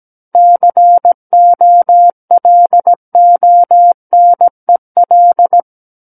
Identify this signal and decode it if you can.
Morse code, heard as a keyed tone, about 17 wpm, 700 Hz. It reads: COLONEL
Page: SKCC Learning Center - Morse Code Practice Files